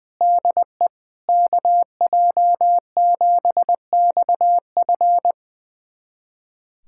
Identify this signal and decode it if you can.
Morse code, heard as a keyed tone, about 20 wpm, 700 Hz. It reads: DE KJ7XF
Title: USA Callsign DE Callsign - 5 Characters - 20wpm